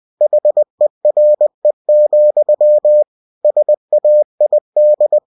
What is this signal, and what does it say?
Morse code, heard as a keyed tone, about 20 wpm, 600 Hz. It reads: HERE, SAID
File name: cw-books / winnie-the-pooh / winnie-20wpm-page-0069